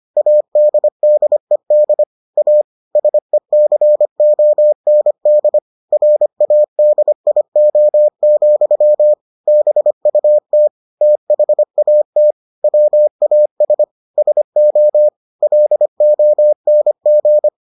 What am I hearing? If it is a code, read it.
Morse code, heard as a keyed tone, about 25 wpm, 600 Hz. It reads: ADDED A SECOND RADIO, BUT THAT WAS SO LONG